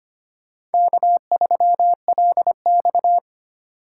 Morse code, heard as a keyed tone, about 25 wpm, 700 Hz. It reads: K3LX
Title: USA Call Signs - 4 Characters - 2x Character Spacing - 25wpm